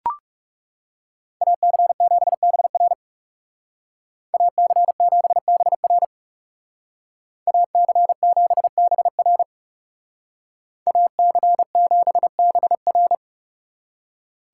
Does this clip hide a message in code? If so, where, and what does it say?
Yes: Morse code from 1.4 to 13.2 s, heard as a keyed tone, about 40 wpm, 700 Hz. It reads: AC7BR AC7BR AC7BR AC7BR